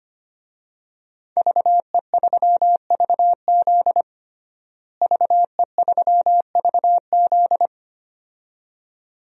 Morse code, heard as a keyed tone, about 25 wpm, 700 Hz. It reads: VE3VZ VE3VZ
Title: DXCC-100 Call Signs - 5 Characters - Repeated 2x - 25wpm